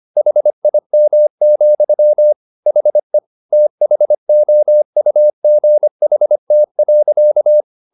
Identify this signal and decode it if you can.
Morse code, heard as a keyed tone, about 25 wpm, 600 Hz. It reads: HIM, HE THOUGHT.